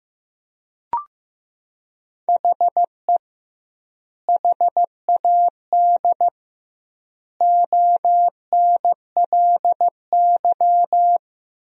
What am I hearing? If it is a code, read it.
Morse code, heard as a keyed tone, about 16 wpm, 700 Hz. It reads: HE HAD ONLY
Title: 2x Word-Spacing - Sentences from Top 300 Words - 15wpm